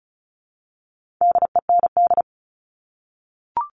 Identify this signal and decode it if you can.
Morse code, heard as a keyed tone, about 34 wpm, 700 Hz. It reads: DEND
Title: All Cut Numbers - 4-Digit 35wpm